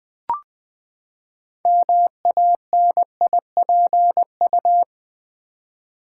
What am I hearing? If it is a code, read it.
Morse code, heard as a keyed tone, about 20 wpm, 700 Hz. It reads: MANIPU
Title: Build A Word - 6 Letters At A Time - 12-Letter Words - 20wpm